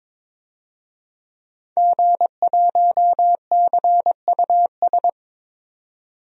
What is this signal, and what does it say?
Morse code, heard as a keyed tone, about 22 wpm, 700 Hz. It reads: G1CUS
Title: DXCC-100 Call Signs - 5 Characters - 22wpm